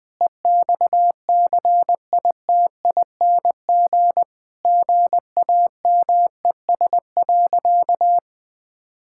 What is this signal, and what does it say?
Morse code, heard as a keyed tone, about 20 wpm, 700 Hz. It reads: EXCITING GAMES.
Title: Sentences with 5-Word Phrases - 20wpm